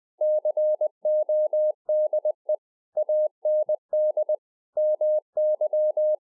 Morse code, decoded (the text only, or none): CODE AND MY